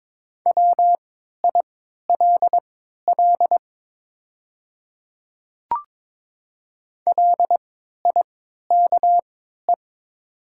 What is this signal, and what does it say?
Morse code, heard as a keyed tone, about 22 wpm, spaced as for 12 wpm, 700 Hz. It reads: WILL LIKE
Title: 3x Character Spacing - Top 100 Words - 22wpm